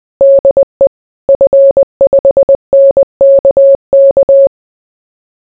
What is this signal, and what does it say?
Morse code, heard as a keyed tone, about 20 wpm, 550 Hz. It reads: DE F5NKK